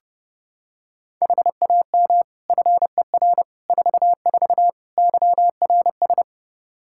Morse code, heard as a keyed tone, about 29 wpm, 700 Hz. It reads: HAM FER 44 YRS
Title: Practice QSO - Exchange 3 - More Info - 30wpm